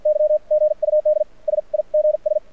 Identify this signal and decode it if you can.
Morse code, heard as a keyed tone, about 55 wpm, spaced as for 44 wpm, 600 Hz. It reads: Y GUD SIGS
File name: archived / CRSnet